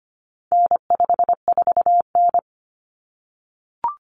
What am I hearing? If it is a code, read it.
Morse code, heard as a keyed tone, about 25 wpm, 700 Hz. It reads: N54N
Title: Common Cut Numbers - 4-Digit 25wpm